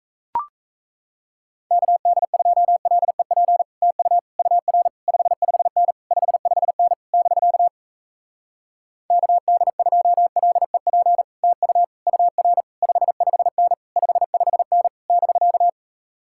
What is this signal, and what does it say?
Morse code, heard as a keyed tone, about 37 wpm, 700 Hz. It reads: KD2LEP TU UR 55N 55N <BK> KD2LEP TU UR 55N 55N <BK>